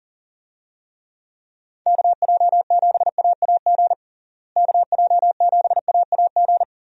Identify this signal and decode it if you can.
Morse code, heard as a keyed tone, about 40 wpm, 700 Hz. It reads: KJ7AAG KJ7AAG
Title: USA Call Signs - 6 Characters - Repeated 2x - 40wpm